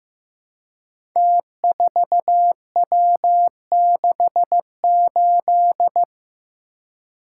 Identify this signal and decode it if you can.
Morse code, heard as a keyed tone, about 15 wpm, 700 Hz. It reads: T4W68
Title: Code Groups - 5 Letter:Numbers - 15wpm